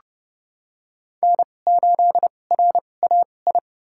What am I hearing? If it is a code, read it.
Morse code, heard as a keyed tone, about 32 wpm, spaced as for 22 wpm, 700 Hz. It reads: N8RAI
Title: USA Call Signs - 5 Characters - 2x Character Spacing - 30wpm